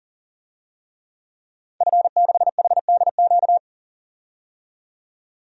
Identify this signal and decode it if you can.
Morse code, heard as a keyed tone, about 40 wpm, 700 Hz. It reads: F6HDQ
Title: DXCC-100 Call Signs - 5 Characters - 40wpm